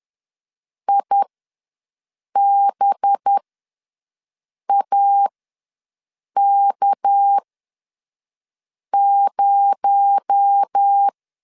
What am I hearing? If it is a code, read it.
Morse code, heard as a keyed tone, about 10 wpm, spaced as for 5 wpm, 800 Hz. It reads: IBAK0
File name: LEZIONI / 6 PKX0 / 6 PKX0 ws 10wpm 4220